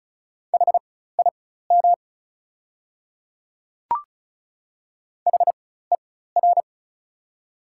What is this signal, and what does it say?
Morse code, heard as a keyed tone, about 36 wpm, spaced as for 16 wpm, 690 Hz. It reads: HIM HER
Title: 4x Character Spacing - Top 100 Words - 35wpm